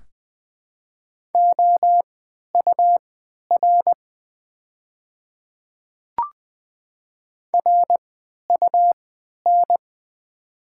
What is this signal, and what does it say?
Morse code, heard as a keyed tone, about 20 wpm, spaced as for 11 wpm, 700 Hz. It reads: OUR RUN
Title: 3x Character Spacing - Top 300 Words - 20wpm